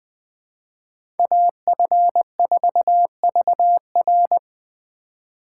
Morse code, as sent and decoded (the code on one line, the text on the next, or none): .- ..-. ....- ...- .-.
AF4VR